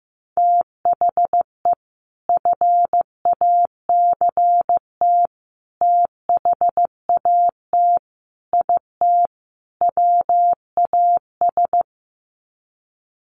Morse code, heard as a keyed tone, about 15 wpm, 700 Hz. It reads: THE FACT THAT IT WAS
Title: Sets of 5 Words Encore2 15wpm